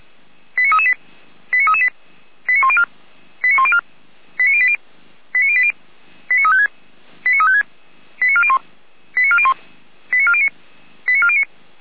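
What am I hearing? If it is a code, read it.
Five-tone selective call (ZVEI), heard as quick runs of five tones, tones of about 70 ms. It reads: 89338 89338 89104 89104 89080 89080 89356 89356 89401 89401 89409 89409
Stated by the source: ZVEI five-tone selective call received by SDR.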